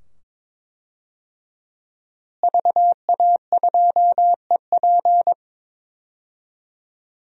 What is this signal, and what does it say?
Morse code, heard as a keyed tone, about 22 wpm, 700 Hz. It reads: VA2EP
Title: DXCC-100 Call Signs - 5 Characters - Repeated 2x - 22wpm